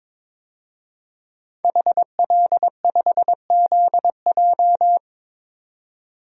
Morse code, heard as a keyed tone, about 22 wpm, 700 Hz. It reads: HL5ZJ
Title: DXCC-100 Call Signs - 5 Characters - 22wpm